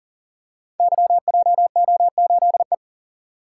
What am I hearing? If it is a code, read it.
Morse code, heard as a keyed tone, about 41 wpm, 690 Hz. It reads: YJO8E
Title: Code Groups - 5 Letter:Numbers - 40wpm